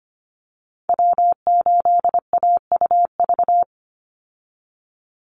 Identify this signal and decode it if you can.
Morse code, heard as a keyed tone, about 25 wpm, 700 Hz. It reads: W8AUV
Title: USA Call Signs 25wpm